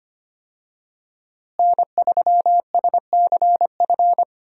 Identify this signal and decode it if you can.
Morse code, heard as a keyed tone, about 25 wpm, 700 Hz. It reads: N3SCF
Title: USA Call Signs - 5 Characters - Repeated 2x - 25wpm